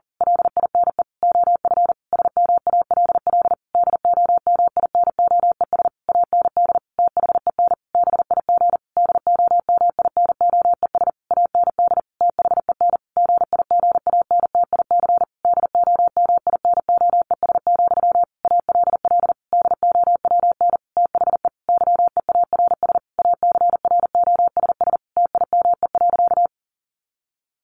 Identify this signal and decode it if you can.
Morse code, heard as a keyed tone, about 40 wpm, 700 Hz. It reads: LINE OF SMALL DOMINOES AND THEN BIG DOMINOES AND THEN GIGANTIC DOMINOES, ALL DOWN THE YEARS ACROSS TIME.
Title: A Sound Of Thunder 40wpm